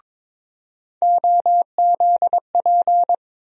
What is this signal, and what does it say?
Morse code, heard as a keyed tone, about 22 wpm, 700 Hz. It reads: OZP
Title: USA Call Sign Postfixes - 3 Letters 22wpm